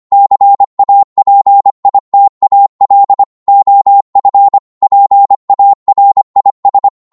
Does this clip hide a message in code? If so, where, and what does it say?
Yes: Morse code from 0.1 to 6.9 s, heard as a keyed tone, about 25 wpm, 800 Hz. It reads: CAPITAL OF PARIS